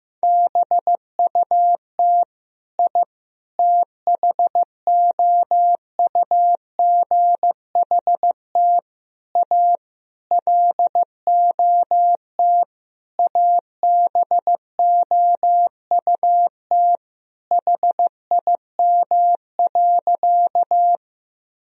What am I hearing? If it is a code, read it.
Morse code, heard as a keyed tone, about 15 wpm, 700 Hz. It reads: BUT I THOUGHT A LOT ABOUT HIM.
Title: Sentences from Top 1000 Words - Review Part 3 - 15wpm